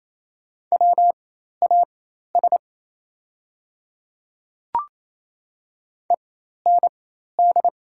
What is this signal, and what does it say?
Morse code, heard as a keyed tone, about 27 wpm, spaced as for 13 wpm, 690 Hz. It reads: WAS END
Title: 4x Character Spacing - Top 200 Words - 28wpm